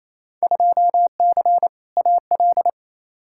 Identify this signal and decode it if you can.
Morse code, heard as a keyed tone, about 28 wpm, 700 Hz. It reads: 2C AL
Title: Field Day Exchanges Encore - 28wpm